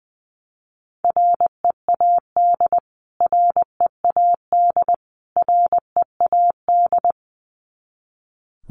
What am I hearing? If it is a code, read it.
Morse code, heard as a keyed tone, about 20 wpm, 700 Hz. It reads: READ READ READ
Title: Top 500 Words Repeated 20wpm